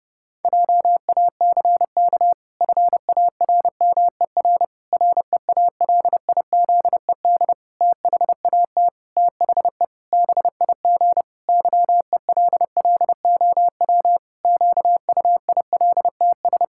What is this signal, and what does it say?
Morse code, heard as a keyed tone, about 30 wpm, 700 Hz. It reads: JACK FARMER REALIZED THAT THE BIG YELLOW QUILTS
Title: Pangram Challenge v2 - 30wpm